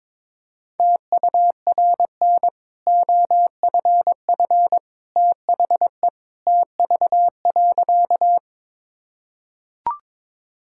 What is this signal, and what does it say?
Morse code, heard as a keyed tone, about 21 wpm, 690 Hz. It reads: TURN OFF THE TV.
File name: practice / 22wpm / Easy Sentences with Hints - 22wpm